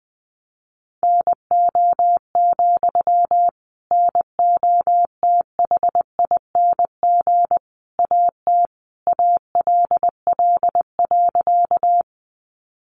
Morse code, heard as a keyed tone, about 20 wpm, 700 Hz. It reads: NO, NOTHING AT ALL.
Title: Sentences from Top 500 Words 20wpm